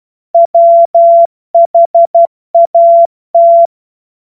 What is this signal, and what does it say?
Morse code, heard as a keyed tone, about 12 wpm, 680 Hz. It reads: WHAT
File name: Morsekurs / Witze